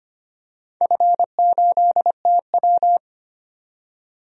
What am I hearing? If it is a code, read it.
Morse code, heard as a keyed tone, about 25 wpm, 700 Hz. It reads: F8TW
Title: DXCC-100 Call Signs - 4 Characters - 25wpm